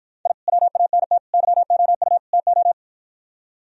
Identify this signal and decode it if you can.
Morse code, heard as a keyed tone, about 53 wpm, 700 Hz. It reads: I WANT YOU TO